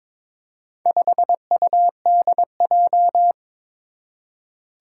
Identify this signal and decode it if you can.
Morse code, heard as a keyed tone, about 22 wpm, 700 Hz. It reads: 5UDJ
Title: Four Challenging Letter-Number Combinations 22wpm